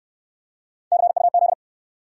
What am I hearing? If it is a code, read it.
Morse code, heard as a keyed tone, about 60 wpm, 700 Hz. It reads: CUZ